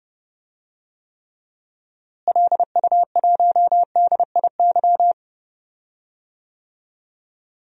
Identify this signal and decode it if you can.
Morse code, heard as a keyed tone, about 30 wpm, 700 Hz. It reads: LU1DIY
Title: DXCC-100 Call Signs - 6 Characters - Repeated 3x - 30wpm